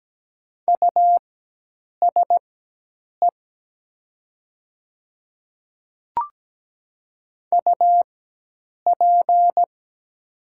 Morse code, heard as a keyed tone, about 17 wpm, spaced as for 8 wpm, 700 Hz. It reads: USE UP
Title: 4x Character Spacing - Top 100 Words - 17wpm